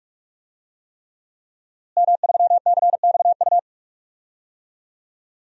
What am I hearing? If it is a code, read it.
Morse code, heard as a keyed tone, about 43 wpm, 700 Hz. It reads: M3CXU
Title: DXCC-100 Call Signs - 5 Characters - Repeated 3x - 45wpm